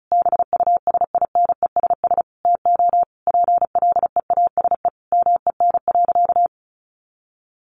Morse code, heard as a keyed tone, about 34 wpm, 700 Hz. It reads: BUSINESS TO PLEASE MEN.
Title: Sentences from Top 1500 Words 35wpm